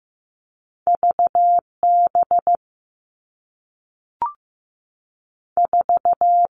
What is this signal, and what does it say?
Morse code, heard as a keyed tone, about 15 wpm, 700 Hz. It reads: V B 4
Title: Two Challenging Letter-Number Combinations 15wpm